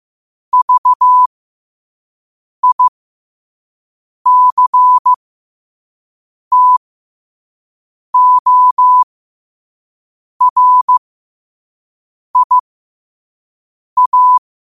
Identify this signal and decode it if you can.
Morse code, heard as a keyed tone, about 15 wpm, spaced as for 5 wpm, 1000 Hz. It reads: VICTORIA
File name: qotdfiles / 5wpm